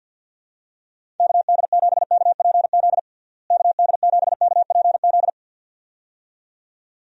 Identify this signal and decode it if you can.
Morse code, heard as a keyed tone, about 52 wpm, 700 Hz. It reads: KD7KPZ KD7KPZ
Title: USA Call Signs - 6 Characters - Repeated 2x - 50wpm